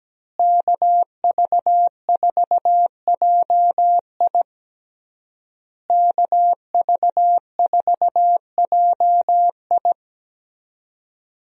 Morse code, heard as a keyed tone, about 17 wpm, 700 Hz. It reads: KV4JI KV4JI
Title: USA Call Signs - 5 Characters - Repeated 3x - 17wpm